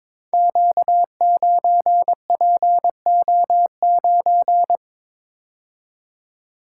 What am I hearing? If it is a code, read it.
Morse code, heard as a keyed tone, about 22 wpm, 700 Hz. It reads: Q9PO9
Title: Code Groups - 5 Letter:Numbers - 22wpm